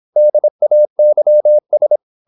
Morse code, heard as a keyed tone, about 27 wpm, 600 Hz. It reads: DAYS
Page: Never 26wpm | CW med Gnister